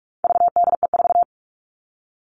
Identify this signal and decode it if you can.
Morse code, heard as a keyed tone, about 44 wpm, 700 Hz. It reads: VDE4
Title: All Cut Numbers - 4-Digit 45wpm